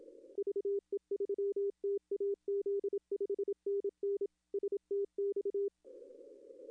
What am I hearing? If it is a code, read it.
Morse code, heard as a keyed tone, about 27 wpm, 400 Hz. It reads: VE3TAZ 5NN STX
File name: contests / arrl-160-k5tr-op-w5tn -0800